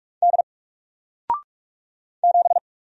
Morse code, heard as a keyed tone, about 44 wpm, 700 Hz. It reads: D 7
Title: Single Character (Letter:Number:Punctuation:Prosign) 45wpm